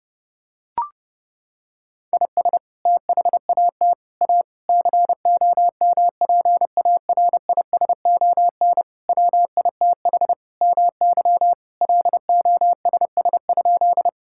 Morse code, heard as a keyed tone, about 30 wpm, 700 Hz. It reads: IS THAT A COMPARISON WITH MY LOSS?
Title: Sentences from Top 1600 Words 30wpm